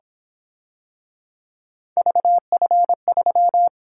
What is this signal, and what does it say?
Morse code, heard as a keyed tone, about 28 wpm, 700 Hz. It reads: VF3